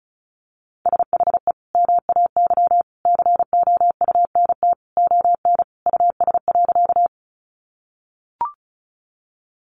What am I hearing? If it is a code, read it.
Morse code, heard as a keyed tone, about 34 wpm, 700 Hz. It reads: SHE MAY COUNT ON US.
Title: Sentences from Top 1500 Words 35wpm